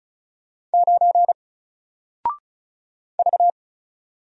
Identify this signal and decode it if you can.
Morse code, heard as a keyed tone, about 36 wpm, 700 Hz. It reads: 9 V
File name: Single-Letter-Number-35wpm